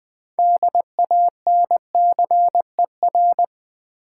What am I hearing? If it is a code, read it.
Morse code, heard as a keyed tone, about 20 wpm, 700 Hz. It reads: DANCER
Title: 6 Letter Words - First 3 Letters Given - 20wpm